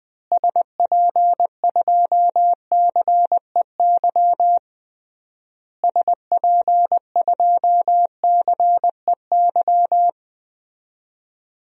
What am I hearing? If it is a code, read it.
Morse code, heard as a keyed tone, about 21 wpm, 700 Hz. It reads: SP2CEY SP2CEY
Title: DXCC-100 Call Signs - 6 Characters - Repeated 3x - 20wpm